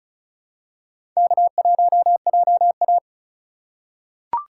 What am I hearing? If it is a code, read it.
Morse code, heard as a keyed tone, about 34 wpm, 700 Hz. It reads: K1JA
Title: USA Call Signs - 4 Characters - 3x Character Spacing - 35wpm